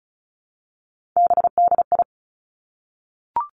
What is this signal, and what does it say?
Morse code, heard as a keyed tone, about 36 wpm, 700 Hz. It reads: BDI
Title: Three Challenging Letter-Number Combinations 35wpm